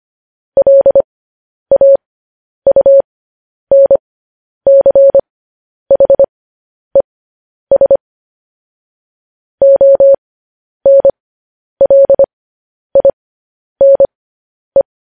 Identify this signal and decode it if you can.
Morse code, heard as a keyed tone, about 25 wpm, spaced as for 10 wpm, 550 Hz. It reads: LAUNCHES ONLINE
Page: news-headlines-in-morse-code-at-10-wpm